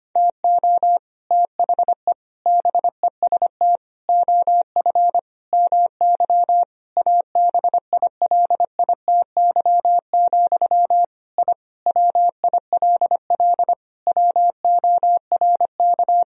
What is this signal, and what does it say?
Morse code, heard as a keyed tone, about 25 wpm, 700 Hz. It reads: TO THE BEST OF MY ABILITY, I WILL WORK
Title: Sentences from Top 1000 Words - Review Part 3 - 25wpm